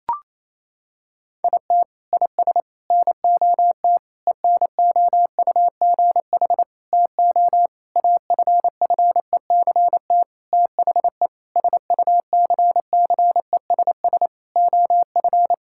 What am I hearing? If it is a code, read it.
Morse code, heard as a keyed tone, about 28 wpm, 700 Hz. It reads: IT IS NOT ENOUGH TO AFFECT THE SUCCESS OF
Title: Sentences from Top 1000 Words - Review Part 6 - 28wpm